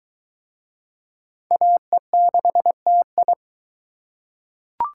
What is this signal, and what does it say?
Morse code, heard as a keyed tone, about 22 wpm, 700 Hz. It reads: AE6TI